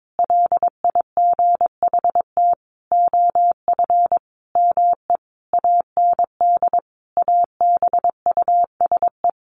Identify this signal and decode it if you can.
Morse code, heard as a keyed tone, about 22 wpm, 700 Hz. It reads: LIGHT OF ME AND ABUSE
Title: Sentences from Top 1300 Words 22wpm